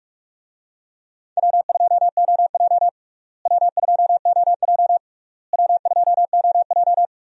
Morse code, heard as a keyed tone, about 44 wpm, 700 Hz. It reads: W2OJ W2OJ W2OJ
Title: USA Call Signs - 4 Characters - Repeated 3x - 45wpm